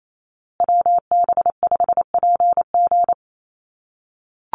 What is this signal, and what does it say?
Morse code, heard as a keyed tone, about 28 wpm, 700 Hz. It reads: WB5PG
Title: USA Call Signs Repeated 2x 28wpm